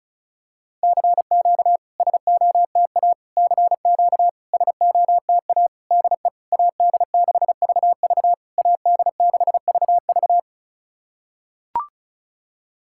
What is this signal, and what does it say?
Morse code, heard as a keyed tone, about 34 wpm, 690 Hz. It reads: CQ SOTA CQ SOTA DE AD6VV AD6VV
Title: SOTA QSO Practice 35wpm